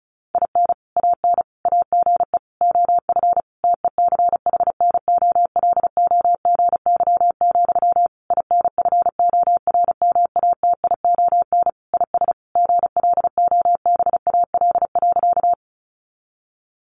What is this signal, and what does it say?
Morse code, heard as a keyed tone, about 34 wpm, 700 Hz. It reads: IN AN AGE OF TECHNOLOGY, INFORMATION IS GLOBAL.
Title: Sentences from Top 1200 Words 35wpm